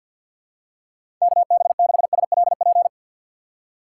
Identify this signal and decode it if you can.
Morse code, heard as a keyed tone, about 48 wpm, 700 Hz. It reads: KB6SLP
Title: USA Call Signs - 6x Character Spacing - 50wpm